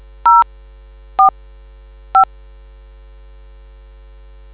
DTMF, heard as telephone touch tones, tones of about 130 ms. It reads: 04 5